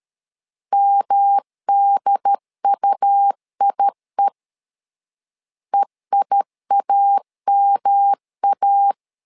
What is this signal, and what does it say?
Morse code, heard as a keyed tone, about 13 wpm, 800 Hz. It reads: MDUIE EIAMA
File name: LEZIONI / 3 MWRO / 3 MW 12wpm 1633